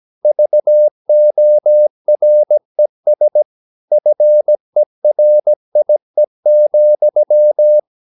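Morse code, heard as a keyed tone, about 17 wpm, 600 Hz. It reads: VORES FERIE,
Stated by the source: Slow (QRS) Morse practice at 17 wpm, from CW med Gnister.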